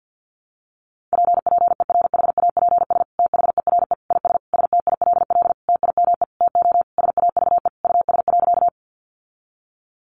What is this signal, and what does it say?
Morse code, heard as a keyed tone, about 50 wpm, 700 Hz. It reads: P PERHAPS THERE IS STILL TIME TO SAVE US.
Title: Sentences from Top 1100 Words 50wpm